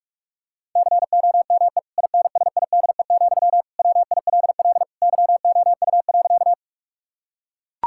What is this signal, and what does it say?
Morse code, heard as a keyed tone, about 41 wpm, 700 Hz. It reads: COME INSIDE, WILL YOU.
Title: Sentences from Top 1000 Words - Review Part 6 - 45wpm